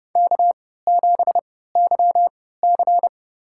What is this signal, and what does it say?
Morse code, heard as a keyed tone, about 31 wpm, spaced as for 17 wpm, 700 Hz. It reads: K7YC
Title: USA Call Signs - 4 Characters - 3x Character Spacing - 30wpm